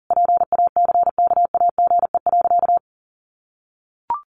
Morse code, heard as a keyed tone, about 37 wpm, 700 Hz. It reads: PACKAGE.
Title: Additional Easy Phrases 40wpm